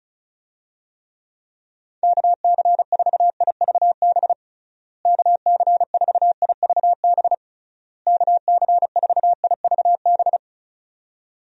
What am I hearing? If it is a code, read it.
Morse code, heard as a keyed tone, about 36 wpm, 700 Hz. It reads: KC4IVB KC4IVB KC4IVB
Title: USA Call Signs - 6 Characters - Repeated 3x - 35wpm